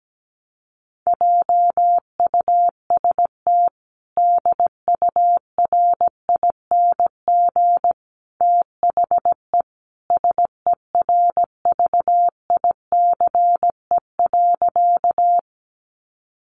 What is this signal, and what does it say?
Morse code, heard as a keyed tone, about 17 wpm, 700 Hz. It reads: JUST DURING THE SERVICE.
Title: Sentences from Top 700 Words 17wpm